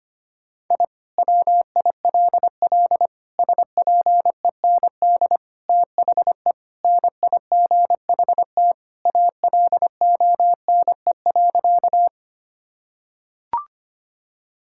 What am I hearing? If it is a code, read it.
Morse code, heard as a keyed tone, about 25 wpm, 700 Hz. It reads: I WILL SPEND THE NIGHT ALONE.
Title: Sentences from Top 1000 Words - Review Part 5 - 25wpm